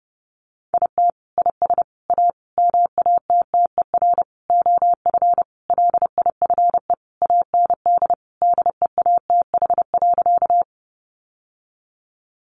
Morse code, heard as a keyed tone, about 30 wpm, 700 Hz. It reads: IT IS A MATTER OF LIFE AND DEATH.
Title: Sentences from Top 1100 Words 30wpm